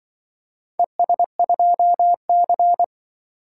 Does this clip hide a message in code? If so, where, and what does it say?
Yes: Morse code from 0.8 to 2.8 s, heard as a keyed tone, about 25 wpm, 700 Hz. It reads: ES2C